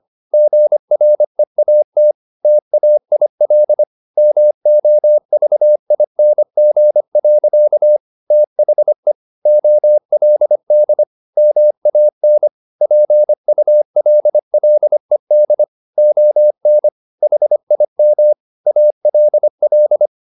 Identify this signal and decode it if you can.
Morse code, heard as a keyed tone, about 25 wpm, 600 Hz. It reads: GREAT TAIL MOVING. THE OLD MAN PULLED ON HIM ALL